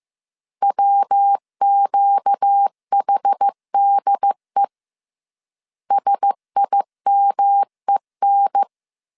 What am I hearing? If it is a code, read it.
Morse code, heard as a keyed tone, about 15 wpm, 800 Hz. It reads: WQHDE SIMEN
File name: CW / DET / 1 6 1500C S B 14wpm 2958